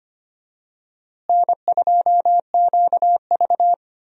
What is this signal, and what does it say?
Morse code, heard as a keyed tone, about 25 wpm, 700 Hz. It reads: N2QV
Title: USA Call Signs - 4 Characters - Repeated 2x - 25wpm